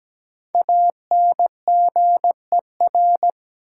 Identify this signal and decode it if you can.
Morse code, heard as a keyed tone, about 17 wpm, 700 Hz. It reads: ANGER
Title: Five Letter Words - 17wpm